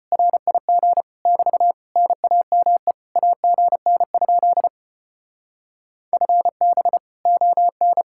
Morse code, heard as a keyed tone, about 33 wpm, 700 Hz. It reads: RIG <BT> NAME AGN? FB ON